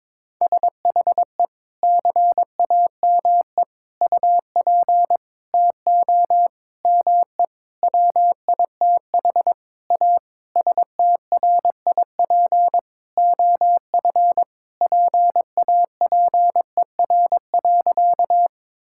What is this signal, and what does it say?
Morse code, heard as a keyed tone, about 22 wpm, 700 Hz. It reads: SHE CAME UP TO ME WITH A STRIP OF PAPER.
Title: Sentences from Top 1600 Words 22wpm